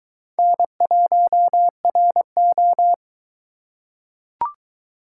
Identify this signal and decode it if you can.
Morse code, heard as a keyed tone, about 22 wpm, 700 Hz. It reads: N1RO